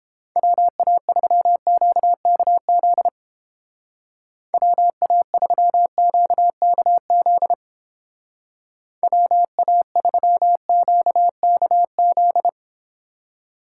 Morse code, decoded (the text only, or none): WA3QKZ WA3QKZ WA3QKZ